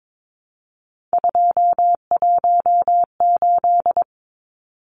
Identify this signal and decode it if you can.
Morse code, heard as a keyed tone, about 22 wpm, 700 Hz. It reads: 218
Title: Numbers - 3-Digit 22wpm